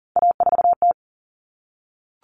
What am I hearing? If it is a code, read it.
Morse code, heard as a keyed tone, about 40 wpm, 700 Hz. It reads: A4T
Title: All Cut Numbers - 3-Digit 40wpm